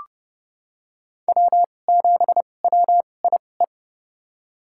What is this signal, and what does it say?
Morse code, heard as a keyed tone, about 32 wpm, spaced as for 22 wpm, 700 Hz. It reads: W7WIE
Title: USA Call Signs - 2x Character Spacing - 30wpm